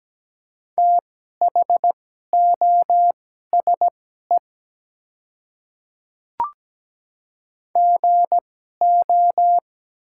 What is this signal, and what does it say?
Morse code, heard as a keyed tone, about 17 wpm, spaced as for 12 wpm, 690 Hz. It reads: THOSE GO
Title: 2x Character Spacing - Top 300 Words - 17wpm